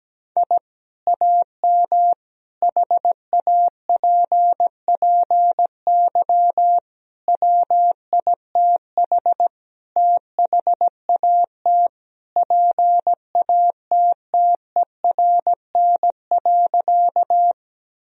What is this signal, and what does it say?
Morse code, heard as a keyed tone, about 17 wpm, 700 Hz. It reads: I AM HAPPY WITH THAT PATTERN.
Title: Sentences from Top 1000 Words - Review Part 2 - 17wpm